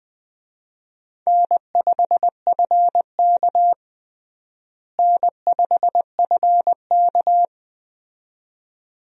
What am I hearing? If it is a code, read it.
Morse code, heard as a keyed tone, about 21 wpm, 700 Hz. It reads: N5FK N5FK
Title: USA Call Signs - 4 Characters - Repeated 2x - 20wpm